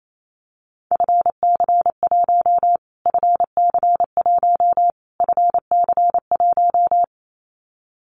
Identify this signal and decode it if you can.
Morse code, heard as a keyed tone, about 28 wpm, 700 Hz. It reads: FC1 FC1 FC1
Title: Callsign Prefixes for Top 25 DXCC - Repeated 3x - 28wpm